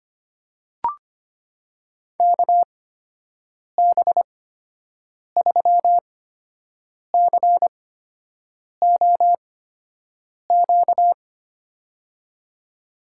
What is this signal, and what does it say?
Morse code, heard as a keyed tone, about 25 wpm, spaced as for 7 wpm, 700 Hz. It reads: KB3COQ
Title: USA Call Signs - 6 Characters - 8x Character Spacing - 25wpm